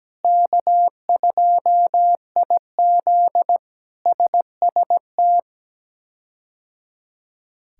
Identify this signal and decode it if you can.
Morse code, heard as a keyed tone, about 17 wpm, 700 Hz. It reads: K2IZ SST
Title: K1USN SST QSO Practice - 17wpm